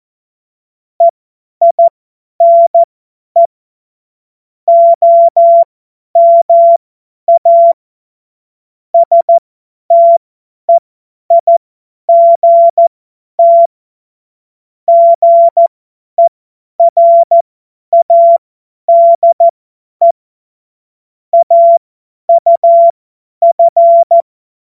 Morse code, heard as a keyed tone, about 14 wpm, spaced as for 10 wpm, 680 Hz. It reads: EINE OMA STEIGT GERADE AUF